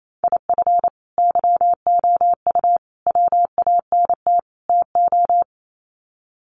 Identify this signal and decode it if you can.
Morse code, heard as a keyed tone, about 28 wpm, 700 Hz. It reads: IF YOU WANT TO
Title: Sets of 4 Words 28wpm